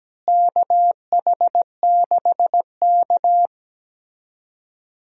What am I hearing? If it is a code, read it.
Morse code, heard as a keyed tone, about 17 wpm, 700 Hz. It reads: KH6K
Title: DXCC-100 Call Signs - 4 Characters - Repeated 3x - 17wpm